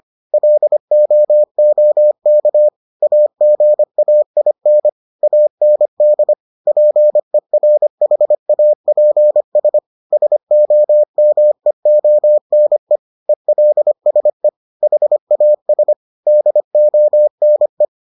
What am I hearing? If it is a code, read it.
Morse code, heard as a keyed tone, about 25 wpm, 600 Hz. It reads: LOOK AGAIN AND PERHAPS SOMEONE ELSE HAS DONE